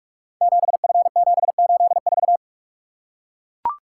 Morse code, heard as a keyed tone, about 44 wpm, 700 Hz. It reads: 7F784
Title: Code Groups - 5 Letter:Numbers - 45wpm